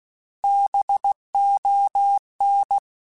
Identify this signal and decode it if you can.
Morse code, heard as a keyed tone, about 16 wpm, 800 Hz. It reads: BON
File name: Alphabetical / Alphabetical mp3s / Bs @15